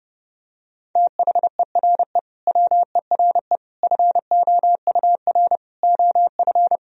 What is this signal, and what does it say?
Morse code, heard as a keyed tone, about 32 wpm, 700 Hz. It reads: THERE WERE FOUR OF
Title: Sentences from Top 1000 Words - Review Part 1 - 30wpm